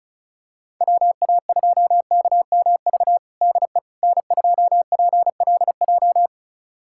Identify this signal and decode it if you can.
Morse code, heard as a keyed tone, about 36 wpm, 700 Hz. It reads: WA2KMV DE N2PLJ
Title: USA Callsign DE Callsign 35wpm